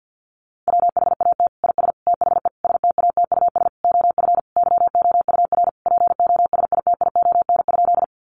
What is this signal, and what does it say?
Morse code, heard as a keyed tone, about 50 wpm, 700 Hz. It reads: WHAT IS THE STATUS OF YOUR POSITION?
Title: Sentences from Top 1400 Words 50wpm